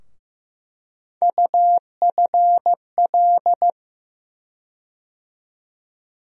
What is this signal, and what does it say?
Morse code, heard as a keyed tone, about 15 wpm, 700 Hz. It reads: UFL
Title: USA Call Sign Postfixes - 3 Letters 15wpm